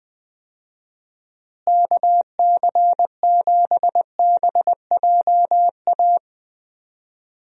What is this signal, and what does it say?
Morse code, heard as a keyed tone, about 21 wpm, 700 Hz. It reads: KC7BJA
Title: USA Call Signs - 6 Characters - Repeated 3x - 20wpm